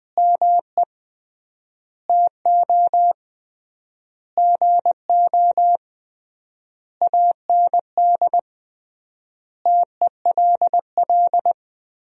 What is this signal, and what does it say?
Morse code, heard as a keyed tone, about 20 wpm, 700 Hz. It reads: ME TO GO AND TELL
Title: 3x Word-Spacing - Sentences from Top 200 Words - 20wpm